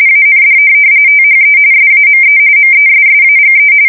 RTTY, about 45.45 baud. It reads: AND THE SAME MESSAGE
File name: fsk170 / printing-telegraph